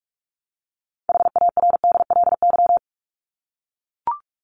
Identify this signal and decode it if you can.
Morse code, heard as a keyed tone, about 44 wpm, 700 Hz. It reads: HARDLY
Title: LY Suffix - Top 100 Words with Base Words 45wpm